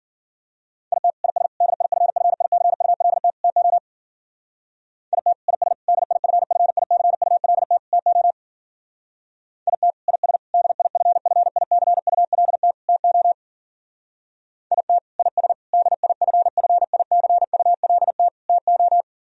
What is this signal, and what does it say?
Morse code, heard as a keyed tone, about 52 wpm, 700 Hz. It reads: IT IS DIFFICULT TO IT IS DIFFICULT TO IT IS DIFFICULT TO IT IS DIFFICULT TO